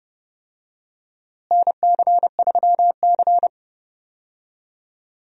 Morse code, heard as a keyed tone, about 31 wpm, 700 Hz. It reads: NC3C
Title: USA Call Signs - 4 Characters - Repeated 3x - 30wpm